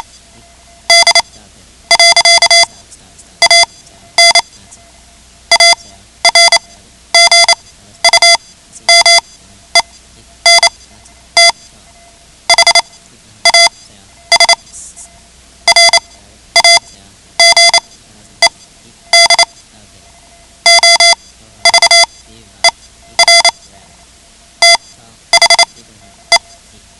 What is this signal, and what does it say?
Morse code, heard as a keyed tone, about 28 wpm, spaced as for 12 wpm, 2050 Hz. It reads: D. AN ARGUMENT HAS RAGED OVER THE
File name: ham radio / CodeQuick / Code Quick 10